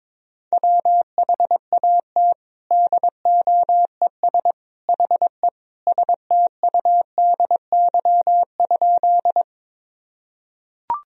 Morse code, heard as a keyed tone, about 22 wpm, 700 Hz. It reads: WHAT DOES HE STUDY?
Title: Sentences from Top 1000 Words - Review Part 1 - 22wpm